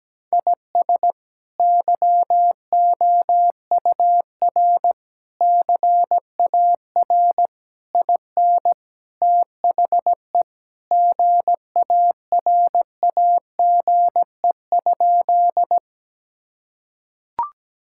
Morse code, heard as a keyed tone, about 17 wpm, 700 Hz. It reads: IS YOUR CAR IN THE GARAGE?
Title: Sentences from Top 1600 Words 17wpm